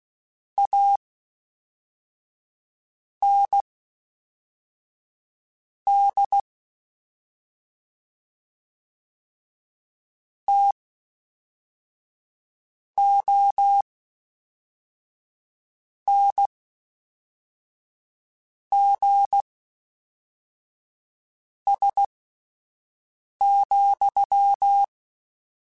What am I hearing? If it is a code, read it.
Morse code, heard as a keyed tone, about 16 wpm, spaced as for 4 wpm, 800 Hz. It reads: AND TONGS,
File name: Misc / Misc-mp3s / WordPairs sp @5